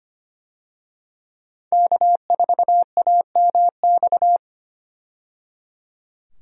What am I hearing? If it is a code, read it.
Morse code, heard as a keyed tone, about 25 wpm, 700 Hz. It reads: K4AMX
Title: USA Call Signs - 5 Characters - 25wpm